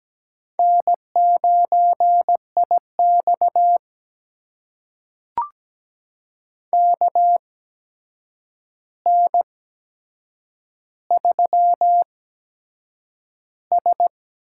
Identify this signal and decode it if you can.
Morse code, heard as a keyed tone, about 17 wpm, spaced as for 7 wpm, 700 Hz. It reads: N9IX KN3S
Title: USA Call Signs - 4 Characters - 8x Character Spacing - 17wpm